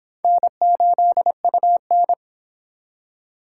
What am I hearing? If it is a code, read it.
Morse code, heard as a keyed tone, about 24 wpm, 700 Hz. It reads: N8UN